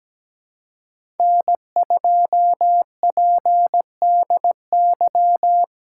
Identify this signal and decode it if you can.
Morse code, heard as a keyed tone, about 17 wpm, 700 Hz. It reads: N2PDY
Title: USA Call Signs - 2x Character Spacing - 17wpm